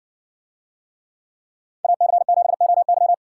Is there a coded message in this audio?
Yes: Morse code from 1.8 to 3.1 s, heard as a keyed tone, about 56 wpm, 700 Hz. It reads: AC7OY